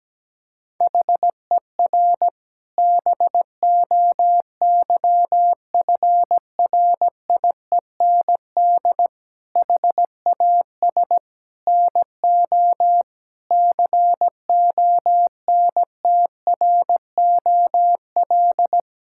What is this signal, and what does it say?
Morse code, heard as a keyed tone, about 17 wpm, 700 Hz. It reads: HER BOYFRIEND HAS NO CONTROL
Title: Sentences from Top 1000 Words - Review Part 5 - 17wpm